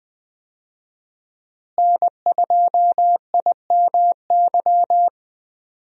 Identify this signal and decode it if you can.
Morse code, heard as a keyed tone, about 21 wpm, 700 Hz. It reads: N2IMY
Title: USA Call Signs - 5 Characters - Repeated 2x - 20wpm